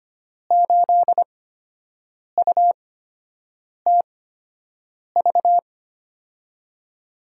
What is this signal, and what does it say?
Morse code, heard as a keyed tone, about 25 wpm, spaced as for 7 wpm, 700 Hz. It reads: 8UTV
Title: USA Call Signs - 6 Characters - 8x Character Spacing - 25wpm